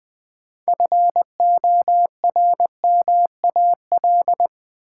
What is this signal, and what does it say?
Morse code, heard as a keyed tone, about 20 wpm, 700 Hz. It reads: FORMAL
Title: Sentences from Top 1300 Words - New Word Drill - 20wpm